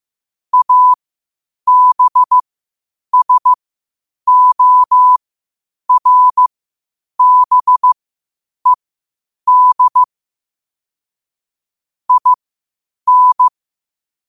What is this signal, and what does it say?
Morse code, heard as a keyed tone, about 15 wpm, spaced as for 8 wpm, 1000 Hz. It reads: ABSORBED IN
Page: qotd-at-8-wpm